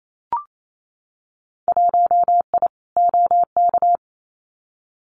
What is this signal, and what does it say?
Morse code, heard as a keyed tone, about 28 wpm, 700 Hz. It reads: 1I OK
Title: Winter Field Day Exchanges Encore 28wpm